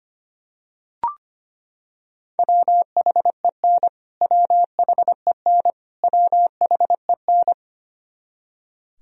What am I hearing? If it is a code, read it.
Morse code, heard as a keyed tone, about 25 wpm, 700 Hz. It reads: WHEN WHEN WHEN
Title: Top 100 Words Repeated 25wpm